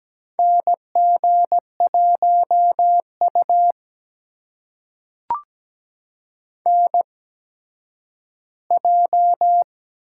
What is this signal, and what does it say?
Morse code, heard as a keyed tone, about 17 wpm, spaced as for 10 wpm, 700 Hz. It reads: NG1U NJ
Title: USA Call Signs - 4 Characters - 8x Character Spacing - 17wpm